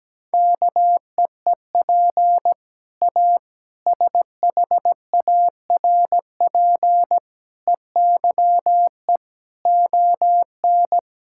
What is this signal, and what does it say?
Morse code, heard as a keyed tone, about 17 wpm, 700 Hz. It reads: KEEP A SHARP EYE ON
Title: Sentences from Top 1600 Words 17wpm